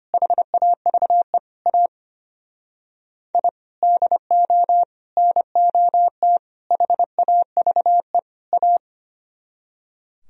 Morse code, heard as a keyed tone, about 31 wpm, 700 Hz. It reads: HAVE A I DO NOT HAVE A
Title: Speed-Racing - Sets of 5 Words 25wpm